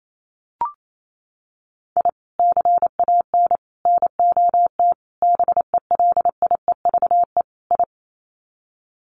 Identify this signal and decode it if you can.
Morse code, heard as a keyed tone, about 28 wpm, 700 Hz. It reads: I CAN NOT BELIEVE I
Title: Sets of 5 Words Encore2 28wpm